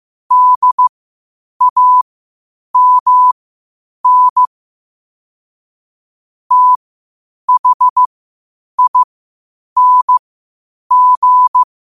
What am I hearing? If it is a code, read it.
Morse code, heard as a keyed tone, about 15 wpm, spaced as for 8 wpm, 1000 Hz. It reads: DAMN THING
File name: qotdfiles / 8wpm